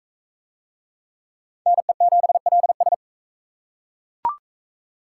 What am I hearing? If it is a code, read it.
Morse code, heard as a keyed tone, about 38 wpm, 700 Hz. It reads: NE7LS